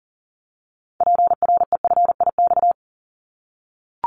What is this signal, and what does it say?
Morse code, heard as a keyed tone, about 38 wpm, 700 Hz. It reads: PREFIX
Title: Callsign Prefixes for Top 25 DXCC - 40wpm